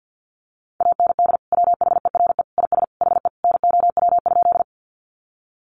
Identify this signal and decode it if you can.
Morse code, heard as a keyed tone, about 50 wpm, 700 Hz. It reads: AND WHERE IS HE NOW?
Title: Sentences from Top 200 Words 50wpm